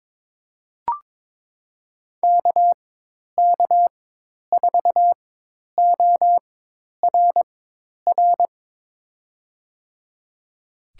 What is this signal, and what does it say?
Morse code, heard as a keyed tone, about 22 wpm, spaced as for 10 wpm, 700 Hz. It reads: KK4ORR
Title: USA Call Signs - 6 Characters - 4x Character Spacing - 22wpm